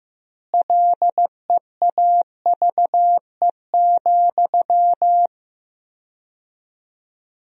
Morse code, heard as a keyed tone, about 15 wpm, 700 Hz. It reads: LEAVE,
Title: 4x Word-Spacing - Sentences from Top 400 Words - 15wpm